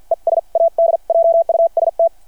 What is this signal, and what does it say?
Morse code, heard as a keyed tone, about 53 wpm, spaced as for 43 wpm, 670 Hz. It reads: ES ADJUST
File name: archived / CRSnet